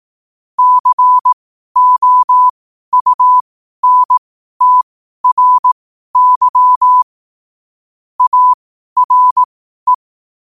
Morse code, heard as a keyed tone, about 18 wpm, spaced as for 12 wpm, 1000 Hz. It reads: COUNTRY ARE
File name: qotdfiles / 12wpm